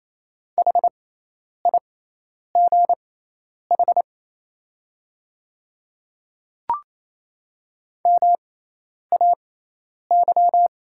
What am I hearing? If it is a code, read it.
Morse code, heard as a keyed tone, about 27 wpm, spaced as for 10 wpm, 700 Hz. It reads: HIGH MAY
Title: 6x Character Spacing - Top 200 Words - 28wpm